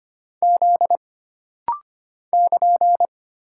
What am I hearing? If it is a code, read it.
Morse code, heard as a keyed tone, about 25 wpm, 700 Hz. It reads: Z <KN>
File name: Single Character (Letter:Number:Punctuation:Prosign) - Rapid-Fire - 25wpm